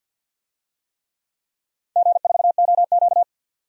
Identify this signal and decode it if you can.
Morse code, heard as a keyed tone, about 52 wpm, 700 Hz. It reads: G4OQ
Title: DXCC-100 Call Signs - 4 Characters - 50wpm